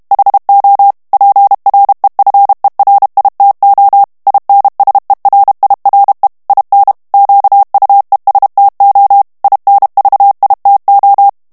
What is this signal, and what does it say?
Morse code, heard as a keyed tone, about 32 wpm, 800 Hz. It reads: HO PREFERITO INSERIRE IN QUESTO INVITO
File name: ITALIEN texte clair 1 vitesse 30 mots minute